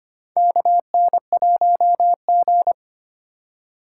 Morse code, heard as a keyed tone, about 25 wpm, 700 Hz. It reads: KN1G
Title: USA Call Signs - 2x Character Spacing - 25wpm